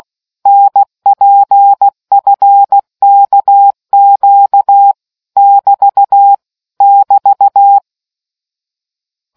Morse code, heard as a keyed tone, about 16 wpm, 800 Hz. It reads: NPFKQ <BT> <BT>